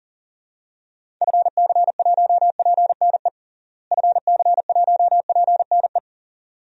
Morse code, heard as a keyed tone, about 37 wpm, 700 Hz. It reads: FC1PNE FC1PNE
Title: DXCC-100 Call Signs - 6 Characters - Repeated 2x - 40wpm